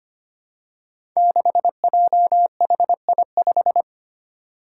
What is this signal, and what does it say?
Morse code, heard as a keyed tone, about 25 wpm, 700 Hz. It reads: 6JHI5
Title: Five Challenging Letter-Number Combinations 25wpm